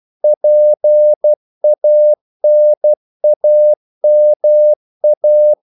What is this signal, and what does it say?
Morse code, heard as a keyed tone, about 12 wpm, 600 Hz. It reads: PANAMA
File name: CW / Practice / NationalCapitals12